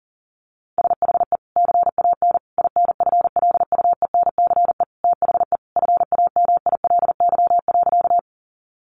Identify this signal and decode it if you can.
Morse code, heard as a keyed tone, about 40 wpm, 700 Hz. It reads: SHE CAN INFLUENCE THE FAMILY.
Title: Sentences from Top 1100 Words 40wpm